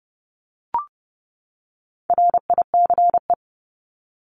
Morse code, heard as a keyed tone, about 30 wpm, 700 Hz. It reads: RICE
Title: Top 1501-1750 Words 30wpm